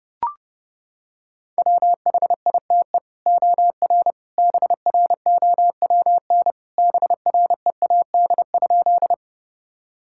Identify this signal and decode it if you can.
Morse code, heard as a keyed tone, about 30 wpm, 700 Hz. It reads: WHITE OR BROWN BREAD?
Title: Sentences from Top 1600 Words 30wpm